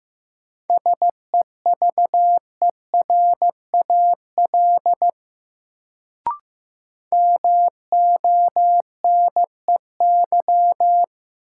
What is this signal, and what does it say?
Morse code, heard as a keyed tone, about 15 wpm, 700 Hz. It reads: SEVERAL MONEY
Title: Sentences from Top 400 Words - New Word Drill - 15wpm